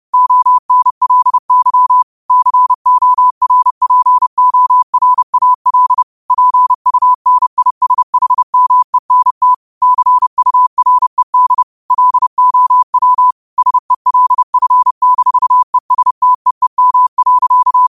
Morse code, heard as a keyed tone, about 30 wpm, 1000 Hz. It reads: ONLY CORPORAL PUNISHMENT CURED LOW SELF-ESTEEM.
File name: qotdfiles / 30wpm